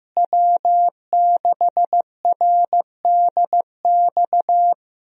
Morse code, heard as a keyed tone, about 16 wpm, 700 Hz. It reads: W6RDX
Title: USA Call Signs - 5 Characters - Repeated 2x - 15wpm